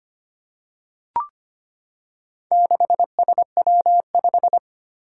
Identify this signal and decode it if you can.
Morse code, heard as a keyed tone, about 25 wpm, 700 Hz. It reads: 6SW5
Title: Four Challenging Letter-Number Combinations 25wpm